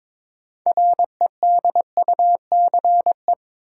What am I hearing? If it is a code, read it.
Morse code, heard as a keyed tone, about 22 wpm, 700 Hz. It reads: REDUCE
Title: Sentences from Top 1100 Words - New Word Drill Alt - 22wpm